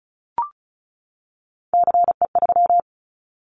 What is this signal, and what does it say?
Morse code, heard as a keyed tone, about 36 wpm, 700 Hz. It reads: CE3
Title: Callsign Prefixes for Top 100 DXCC - 35wpm